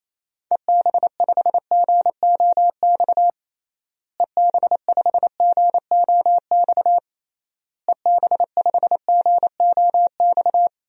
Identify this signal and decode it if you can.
Morse code, heard as a keyed tone, about 27 wpm, 700 Hz. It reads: EB5GOX EB5GOX EB5GOX
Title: DXCC-100 Call Signs - 6 Characters - Repeated 3x - 28wpm